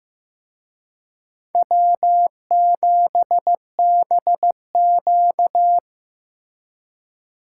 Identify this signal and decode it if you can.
Morse code, heard as a keyed tone, about 18 wpm, 700 Hz. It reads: W7BQ
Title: Speed Racing - USA 1x2 Call Signs 15wpm